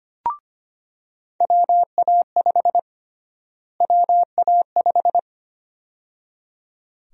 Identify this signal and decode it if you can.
Morse code, heard as a keyed tone, about 25 wpm, 700 Hz. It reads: WA5 WA5
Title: USA Call Sign Prefixes - Repeated 2x - 25wpm